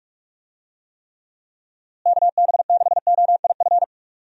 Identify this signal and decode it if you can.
Morse code, heard as a keyed tone, about 46 wpm, 700 Hz. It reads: KB6OIF
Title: USA Call Signs - 6 Characters - Repeated 2x - 45wpm